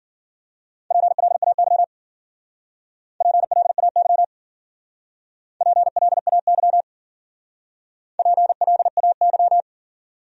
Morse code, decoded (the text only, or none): PLAY PLAY PLAY PLAY